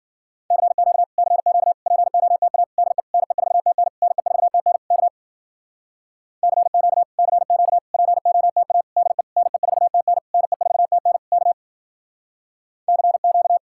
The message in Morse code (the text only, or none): CQ CQ POTA DE NE4TN NE4TN K CQ CQ POTA DE NE4TN NE4TN K CQ